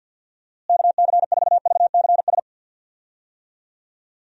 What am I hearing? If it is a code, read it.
Morse code, heard as a keyed tone, about 53 wpm, 700 Hz. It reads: KC4VCS
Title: USA Call Signs - 6 Characters - Repeated 2x - 50wpm